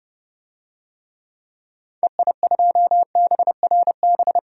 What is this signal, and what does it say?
Morse code, heard as a keyed tone, about 30 wpm, 700 Hz. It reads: EI2BRB
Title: DXCC-100 Call Signs - 6 Characters - 30wpm